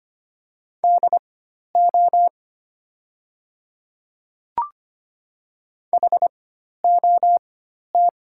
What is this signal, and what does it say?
Morse code, heard as a keyed tone, about 25 wpm, spaced as for 12 wpm, 700 Hz. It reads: DO HOT
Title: 4x Character Spacing - Top 100 Words - 25wpm